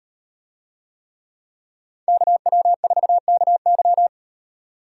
Morse code, heard as a keyed tone, about 38 wpm, 700 Hz. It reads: KW4KY